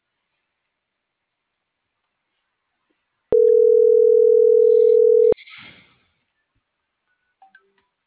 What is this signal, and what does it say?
Ringback tone (North America), 440 + 480 Hz.